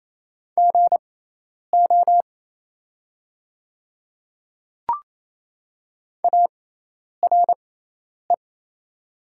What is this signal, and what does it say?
Morse code, heard as a keyed tone, about 27 wpm, spaced as for 10 wpm, 700 Hz. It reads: GO ARE
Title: 6x Character Spacing - Top 100 Words - 28wpm